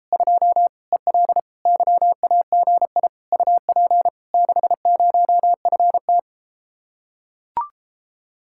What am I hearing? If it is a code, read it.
Morse code, heard as a keyed tone, about 33 wpm, 700 Hz. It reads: 2 EL YAGI UP 60FT